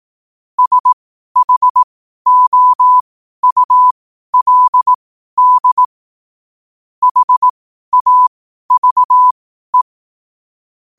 Morse code, heard as a keyed tone, about 18 wpm, spaced as for 12 wpm, 1000 Hz. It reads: SHOULD HAVE
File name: qotdfiles / 12wpm